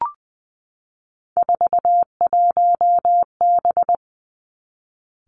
Morse code, heard as a keyed tone, about 20 wpm, 700 Hz. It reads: 41B
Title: Three Challenging Letter-Number Combinations 20wpm